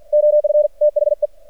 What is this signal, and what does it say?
Morse code, heard as a keyed tone, about 53 wpm, 600 Hz. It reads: OF THE
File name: archived / CRSnet